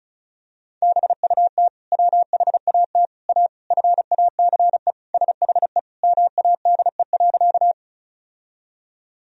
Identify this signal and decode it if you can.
Morse code, heard as a keyed tone, about 34 wpm, 700 Hz. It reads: BUT WHAT A FACE SHE MADE.
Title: Sentences from Top 1000 Words - Review Part 2 - 35wpm